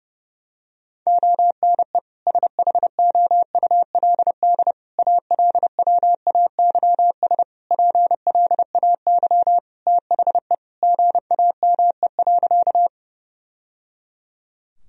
Morse code, heard as a keyed tone, about 30 wpm, 700 Hz. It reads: ONE SHOULD ALWAYS PLAY THE GAME.
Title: Sentences from Top 1000 Words - Review Part 3 - 30wpm